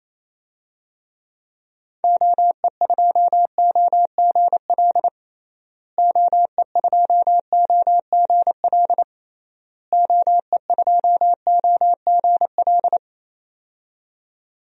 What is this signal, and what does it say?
Morse code, heard as a keyed tone, about 28 wpm, 700 Hz. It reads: OE2OGL OE2OGL OE2OGL
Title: DXCC-100 Call Signs - 6 Characters - Repeated 3x - 28wpm